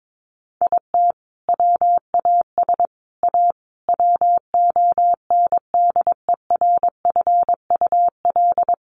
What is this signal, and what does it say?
Morse code, heard as a keyed tone, about 22 wpm, 700 Hz. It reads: IT WAS A WONDERFUL
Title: Sentences from Top 1500 Words 22wpm